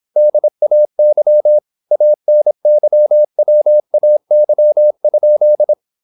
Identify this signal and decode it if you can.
Morse code, heard as a keyed tone, about 26 wpm, 600 Hz. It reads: DAY ANYWAY?
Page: Never 26wpm | CW med Gnister